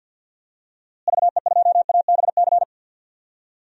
Morse code, heard as a keyed tone, about 52 wpm, 700 Hz. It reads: VE2ABC
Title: DXCC-100 Call Signs - 6 Characters - Repeated 3x - 50wpm